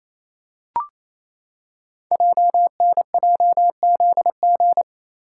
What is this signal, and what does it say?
Morse code, heard as a keyed tone, about 27 wpm, 700 Hz. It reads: JNJZG
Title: Code Groups - 5 Letters - 28wpm